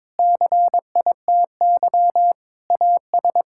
Morse code, heard as a keyed tone, about 22 wpm, 700 Hz. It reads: CITY AS
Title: Pangram Challenge v2 - 22wpm